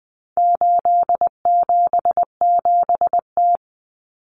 Morse code, heard as a keyed tone, about 20 wpm, 700 Hz. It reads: 877T
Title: Common Cut Numbers - 4-Digit 20wpm